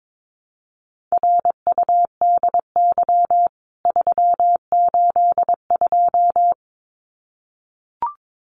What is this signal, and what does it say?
Morse code, heard as a keyed tone, about 22 wpm, 700 Hz. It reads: RUDY 382
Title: CWT Exchange 22wpm